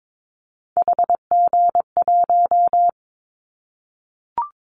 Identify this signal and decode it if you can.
Morse code, heard as a keyed tone, about 22 wpm, 700 Hz. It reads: HG1
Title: Callsign Prefixes for Top 25 DXCC - Repeated 3x - 22wpm